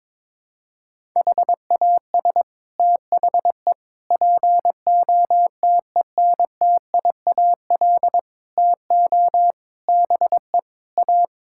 Morse code, heard as a keyed tone, about 22 wpm, 700 Hz. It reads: HAS THE POTENTIAL TO BE A
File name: practice / 22wpm / Sets of 6 Words - 22wpm